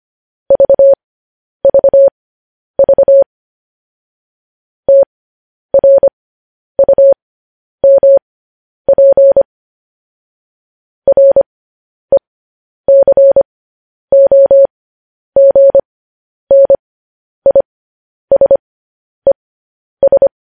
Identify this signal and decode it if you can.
Morse code, heard as a keyed tone, about 25 wpm, spaced as for 10 wpm, 550 Hz. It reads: VVV TRUMP RECOGNISES
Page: news headlines in morse code at 10 wpm